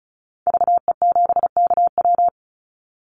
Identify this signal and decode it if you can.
Morse code, heard as a keyed tone, about 34 wpm, 700 Hz. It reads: VE7KW
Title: Call Signs Repeated 2x - 35wpm